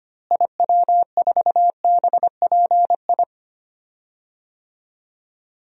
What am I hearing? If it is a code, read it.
Morse code, heard as a keyed tone, about 25 wpm, 700 Hz. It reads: IW4BPI
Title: DXCC-100 Call Signs - 6 Characters - Repeated 2x - 25wpm